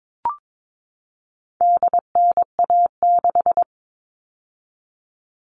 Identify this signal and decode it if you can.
Morse code, heard as a keyed tone, about 22 wpm, 700 Hz. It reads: DNA6
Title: All Cut Numbers - 4-Digit 22wpm